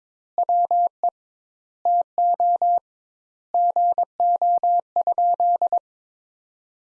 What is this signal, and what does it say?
Morse code, heard as a keyed tone, about 21 wpm, 700 Hz. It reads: WE TO GO?
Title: Sentences Top 200 Words - 22wpm